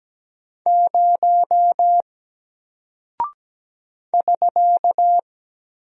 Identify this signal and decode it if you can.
Morse code, heard as a keyed tone, about 17 wpm, 700 Hz. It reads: 0 <SK>
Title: Single Character (Letter:Number:Punctuation:Prosign) Alt 17wpm